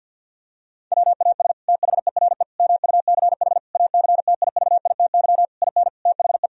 Morse code, heard as a keyed tone, about 50 wpm, 700 Hz. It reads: WAS THERE MUCH ACTIVITY IN THE
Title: Sentences from Top 1000 Words - Review Part 4 - 50wpm